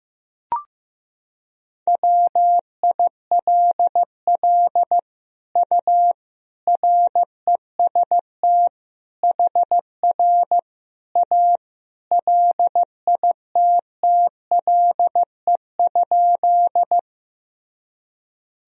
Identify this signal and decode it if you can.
Morse code, heard as a keyed tone, about 15 wpm, 700 Hz. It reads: WILL U REST HR A LITTLE?
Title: Sentences from Top 500 Words and Abbreviations 15wpm